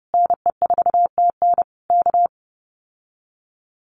Morse code, heard as a keyed tone, about 30 wpm, 700 Hz. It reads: NE4TN K
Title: Common QSO Phrases Encore 30wpm